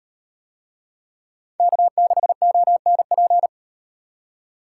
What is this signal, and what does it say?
Morse code, heard as a keyed tone, about 35 wpm, 700 Hz. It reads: K6ONP